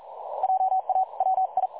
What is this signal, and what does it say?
Morse code, heard as a keyed tone, about 42 wpm, 760 Hz. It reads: JA WI
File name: dlazaawansowanych / nagraniazpasma / websdr recording 3565.0kHz